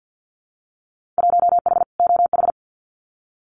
Morse code, heard as a keyed tone, about 50 wpm, 700 Hz. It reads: 1H OH
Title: Winter Field Day Exchanges 50wpm